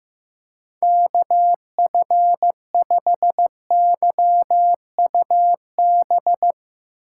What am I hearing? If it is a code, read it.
Morse code, heard as a keyed tone, about 15 wpm, 700 Hz. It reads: KF5YUB
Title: K1USN SST QSO Practice - 15wpm